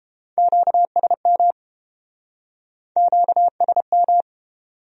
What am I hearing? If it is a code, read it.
Morse code, heard as a keyed tone, about 31 wpm, 700 Hz. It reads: QSM QSM